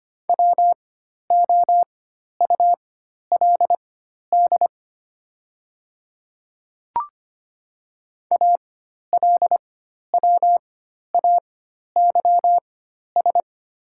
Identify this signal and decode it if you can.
Morse code, heard as a keyed tone, about 25 wpm, spaced as for 12 wpm, 690 Hz. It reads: WOULD ALWAYS
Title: 4x Character Spacing - Top 500 Words - 25wpm